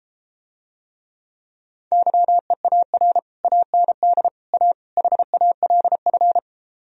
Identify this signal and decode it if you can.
Morse code, heard as a keyed tone, about 33 wpm, 700 Hz. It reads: YEAR AND A HALF